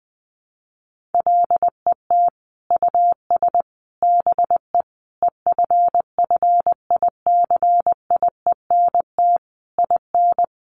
Morse code, heard as a keyed tone, about 20 wpm, 700 Hz. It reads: LET US BE EFFICIENT IN
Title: Sentences from Top 1300 Words 20wpm